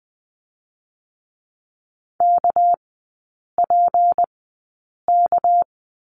Morse code, heard as a keyed tone, about 20 wpm, 700 Hz. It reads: K P K
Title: 198 Lesson 32e - Letter k - Three Characters 20wpm